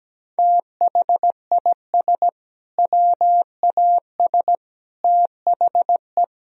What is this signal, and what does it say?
Morse code, heard as a keyed tone, about 17 wpm, 700 Hz. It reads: THIS WAS THE
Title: Sentences from Top 1000 Words - Review Part 1 - 17wpm